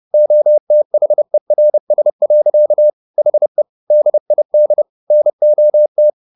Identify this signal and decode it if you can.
Morse code, heard as a keyed tone, about 30 wpm, 600 Hz. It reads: OTHERS. HE DID NOT